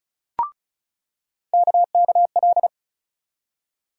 Morse code, heard as a keyed tone, about 36 wpm, 700 Hz. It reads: KKL
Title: USA Call Sign Postfixes - 3 Letters 35wpm